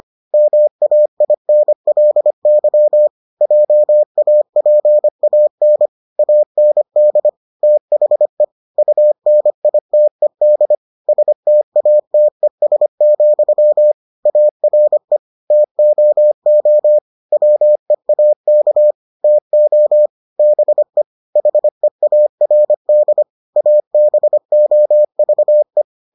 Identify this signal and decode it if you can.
Morse code, heard as a keyed tone, about 25 wpm, 600 Hz. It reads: MAINLY JAPAN AND THE UNITED STATES, ARE TOO WEAK TO BE HEARD ABOVE